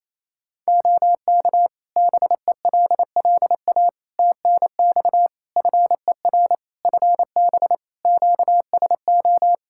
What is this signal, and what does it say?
Morse code, heard as a keyed tone, about 27 wpm, 700 Hz. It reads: OK BELLA TNX FER FB QSO
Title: K1USN SST QSO Practice 28wpm